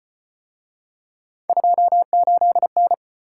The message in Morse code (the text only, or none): 28N